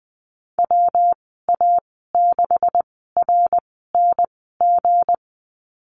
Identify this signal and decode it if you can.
Morse code, heard as a keyed tone, about 20 wpm, spaced as for 15 wpm, 700 Hz. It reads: WA6RNG
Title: 179 Lesson 29g - Slash - Call Signs 20wpm